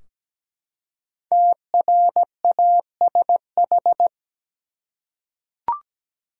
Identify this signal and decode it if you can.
Morse code, heard as a keyed tone, about 17 wpm, 700 Hz. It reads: TRASH
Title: Five Letter Words - 17wpm